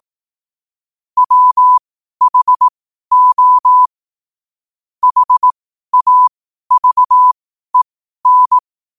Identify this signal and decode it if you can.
Morse code, heard as a keyed tone, about 18 wpm, spaced as for 12 wpm, 1000 Hz. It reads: WHO HAVEN
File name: qotdfiles / 12wpm